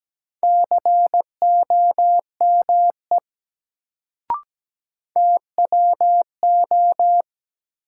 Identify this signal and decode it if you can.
Morse code, heard as a keyed tone, about 17 wpm, 700 Hz. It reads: COME TWO
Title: Sentences from Top 100 Words - New Word Drill - 17wpm